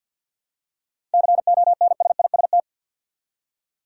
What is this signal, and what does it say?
Morse code, heard as a keyed tone, about 52 wpm, 700 Hz. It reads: CONSIST
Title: Seven Letter Words - 50wpm